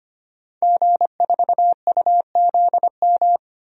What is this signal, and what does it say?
Morse code, heard as a keyed tone, about 25 wpm, 700 Hz. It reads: G4UZM
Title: DXCC-100 Call Signs - 5 Characters - 25wpm